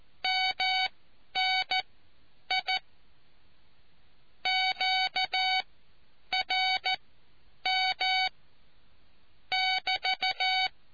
Morse code, heard as a keyed tone, about 14 wpm, spaced as for 9 wpm, 750 Hz. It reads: MNI QRM <BT>